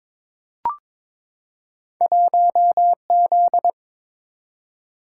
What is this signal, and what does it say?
Morse code, heard as a keyed tone, about 21 wpm, 700 Hz. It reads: 1Z
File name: practice / 22wpm / Two Letter-Number Combinations - 22wpm